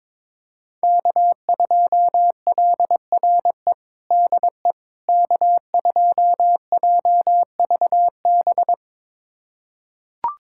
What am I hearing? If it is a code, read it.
Morse code, heard as a keyed tone, about 22 wpm, 700 Hz. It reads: K2LRE DE K2JVB
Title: USA Callsign DE Callsign - 5 Characters - 22wpm